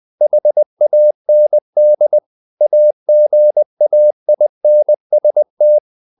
Morse code, heard as a keyed tone, about 20 wpm, 600 Hz. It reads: HAND AGAINST
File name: the old man and the sea part 2 20wpm